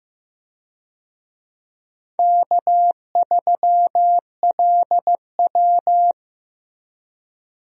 Morse code, heard as a keyed tone, about 15 wpm, 700 Hz. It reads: K3LW
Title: USA Call Signs - 4 Characters - Repeated 3x - 15wpm